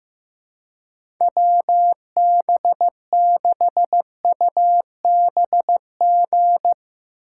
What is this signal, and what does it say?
Morse code, heard as a keyed tone, about 15 wpm, 700 Hz. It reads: WB6UBG
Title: USA Call Signs - 6 Characters - 3x Character Spacing - 15wpm